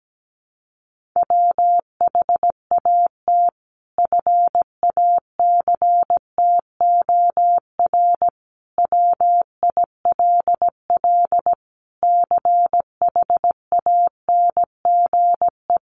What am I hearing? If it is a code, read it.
Morse code, heard as a keyed tone, about 17 wpm, 700 Hz. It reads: WHAT FACTOR WILL CHANGE
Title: Sentences from Top 1300 Words 17wpm